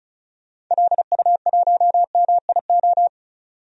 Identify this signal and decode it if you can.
Morse code, heard as a keyed tone, about 36 wpm, 700 Hz. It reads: LU1MIO
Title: DXCC-100 Call Signs - 6 Characters - 35wpm